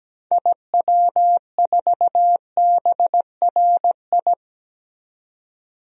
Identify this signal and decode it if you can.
Morse code, heard as a keyed tone, about 17 wpm, 700 Hz. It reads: IW4BRI
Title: DXCC-100 Call Signs - 6 Characters - 17wpm